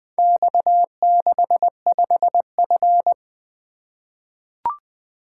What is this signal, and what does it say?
Morse code, heard as a keyed tone, about 20 wpm, 700 Hz. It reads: X65F
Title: Four Letter-Number Combinations - 20wpm